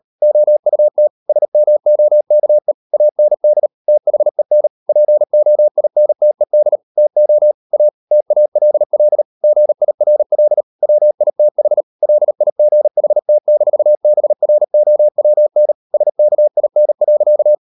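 Morse code, heard as a keyed tone, about 38 wpm, 600 Hz. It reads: OUT SMOKE AND THEN POINTED TO A TALL GIRL WITH LIGHT-BROWN SKIN.
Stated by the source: Morse code practice at 38 WPM, sending Ken Follett's novel Never in English, from CW med Gnister.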